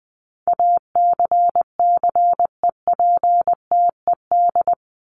Morse code, heard as a keyed tone, about 19 wpm, 700 Hz. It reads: ACCEPTED
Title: ED Suffix - Top 100 Words with Base Words 20wpm